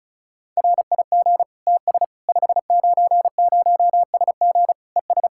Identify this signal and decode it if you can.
Morse code, heard as a keyed tone, about 36 wpm, 700 Hz. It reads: RIG TS 590SG ES
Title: Practice QSO - Exchange 2 - Info - 35wpm